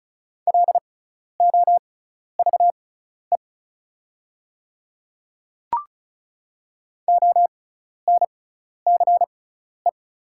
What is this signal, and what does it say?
Morse code, heard as a keyed tone, about 36 wpm, spaced as for 12 wpm, 690 Hz. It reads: LOVE ONCE
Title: 6x Character Spacing - Top 500 Words - 35wpm